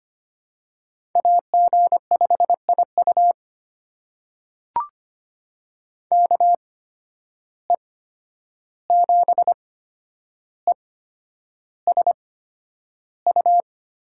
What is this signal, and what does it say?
Morse code, heard as a keyed tone, about 25 wpm, spaced as for 10 wpm, 700 Hz. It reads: AG5IU KE7ESU
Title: USA Call Signs - 8x Character Spacing - 25wpm